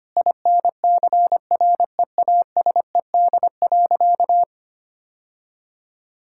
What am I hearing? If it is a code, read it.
Morse code, heard as a keyed tone, about 25 wpm, 700 Hz. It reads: INCREASED.
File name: practice / 25wpm / Sentences with 3-Word Phrases - 25wpm